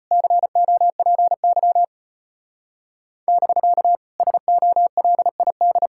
Morse code, heard as a keyed tone, about 30 wpm, 700 Hz. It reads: COPY <BK> SOLID
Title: Speed-Racing Common QSO Phrases - 25wpm